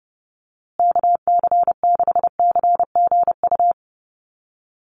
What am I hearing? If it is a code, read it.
Morse code, heard as a keyed tone, about 29 wpm, 700 Hz. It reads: KC6CGU
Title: USA Call Signs Repeated 2x 30wpm